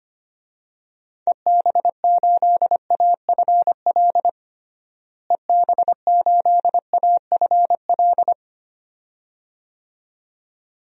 Morse code, heard as a keyed tone, about 25 wpm, 700 Hz. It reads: EB8AFL EB8AFL
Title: DXCC-100 Call Signs - 6 Characters - Repeated 2x - 25wpm